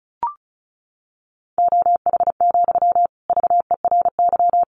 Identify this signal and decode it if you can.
Morse code, heard as a keyed tone, about 34 wpm, 700 Hz. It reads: OH, VERY
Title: Sentences from Top 1000 Words 35wpm